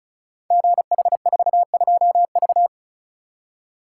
Morse code, heard as a keyed tone, about 36 wpm, 700 Hz. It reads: GH42V
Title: Code Groups - 5 Letter:Numbers - 35wpm